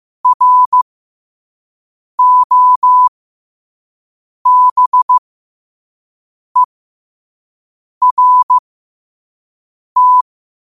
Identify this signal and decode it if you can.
Morse code, heard as a keyed tone, about 15 wpm, spaced as for 5 wpm, 1000 Hz. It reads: ROBERT
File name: qotdfiles / 5wpm